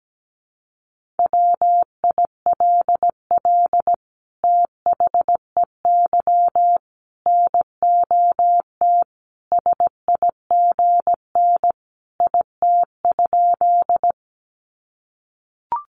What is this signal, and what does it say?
Morse code, heard as a keyed tone, about 17 wpm, 700 Hz. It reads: WILL THEY NOT SIGN IT?
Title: Sentences from Top 1000 Words 17wpm